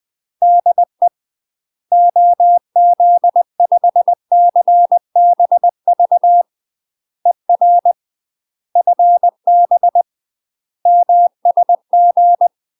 Morse code, heard as a keyed tone, about 20 wpm, 700 Hz. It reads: DE OZ5CBV ER FB MSG
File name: UK MASTER QSOs TRAINING/20WPM